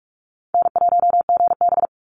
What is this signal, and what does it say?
Morse code, heard as a keyed tone, about 46 wpm, 690 Hz. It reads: N1GB
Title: NA SOTA Chaser Callsigns 45wpm